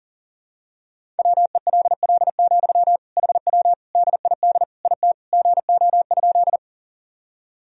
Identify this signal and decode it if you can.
Morse code, heard as a keyed tone, about 40 wpm, 700 Hz. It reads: WELL, HW DID IT GO?
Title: Sentences from Top 200 Words and Abbreviations 40wpm